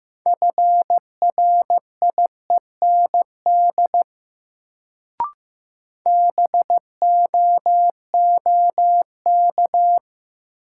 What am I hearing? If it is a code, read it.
Morse code, heard as a keyed tone, about 15 wpm, 700 Hz. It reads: FRIEND BOOK
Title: Sentences from Top 300 Words - New Word Drill - 15wpm